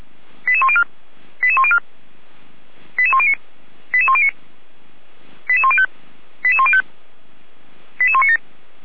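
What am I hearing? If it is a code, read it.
Five-tone selective call (ZVEI), heard as quick runs of five tones, tones of about 70 ms. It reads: 88104 88104 88109 88109 88105 88105 88197
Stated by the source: Received on SDR as ZVEI five-tone selective call.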